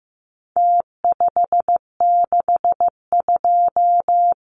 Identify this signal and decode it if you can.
Morse code, heard as a keyed tone, about 15 wpm, 700 Hz. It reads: T562
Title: Numbers - Contest Style - 4-Digit 15wpm